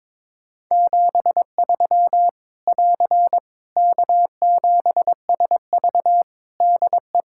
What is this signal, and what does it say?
Morse code, heard as a keyed tone, about 22 wpm, 700 Hz. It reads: 73 <AR> K7SV DE
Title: Practice QSO - Exchange 4 - Ending - 22wpm